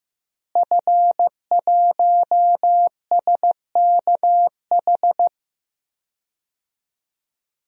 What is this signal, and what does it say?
Morse code, heard as a keyed tone, about 16 wpm, 700 Hz. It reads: F1SKH
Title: DXCC-100 Call Signs - 5 Characters - Repeated 3x - 15wpm